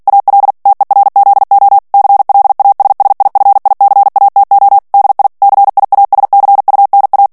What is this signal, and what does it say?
Morse code, heard as a keyed tone, about 48 wpm, 800 Hz. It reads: AL TERZO CLASSIFICATO DI CIASCUNA
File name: ITALIEN texte clair 2 vitesse 45 mots minute